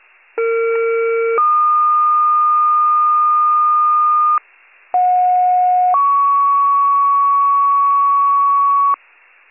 Two-tone page, heard as a one-second tone, then a three-second tone, about 460 then 1220 Hz, about 730 then 1120 Hz. It is heard from 0.4 to 9.0 s.